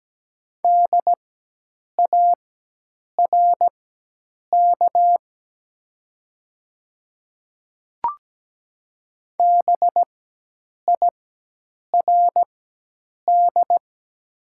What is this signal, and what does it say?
Morse code, heard as a keyed tone, about 17 wpm, spaced as for 8 wpm, 700 Hz. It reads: DARK BIRD
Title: 4x Character Spacing - Top 500 Words - 17wpm